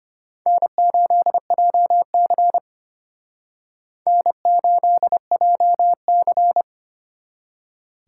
Morse code, heard as a keyed tone, about 32 wpm, 700 Hz. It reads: N8JC N8JC